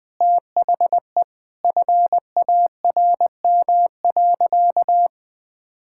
Morse code, heard as a keyed tone, about 21 wpm, 700 Hz. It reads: THE FARM.
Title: Sentences from Top 1000 Words - Review Part 6 - 20wpm